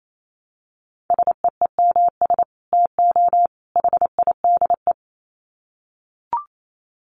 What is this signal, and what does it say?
Morse code, heard as a keyed tone, about 28 wpm, 700 Hz. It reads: SEEMS TO HIDE
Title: Dit Challenge 28wpm